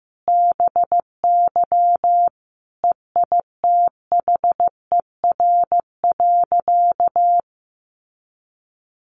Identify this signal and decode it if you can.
Morse code, heard as a keyed tone, about 15 wpm, 700 Hz. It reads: BY EITHER.
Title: Sentences from Top 600 Words 15wpm